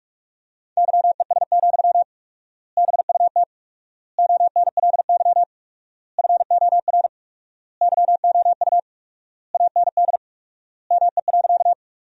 Morse code, heard as a keyed tone, about 41 wpm, 700 Hz. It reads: YES, BUT ONLY FOR YOU AND ME.
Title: 4x Word-Spacing - Sentences from Top 500 Words - 45wpm